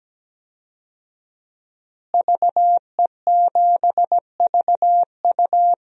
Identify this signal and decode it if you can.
Morse code, heard as a keyed tone, about 17 wpm, 700 Hz. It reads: VE7VU
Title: DXCC-100 Call Signs - 5 Characters - Repeated 3x - 17wpm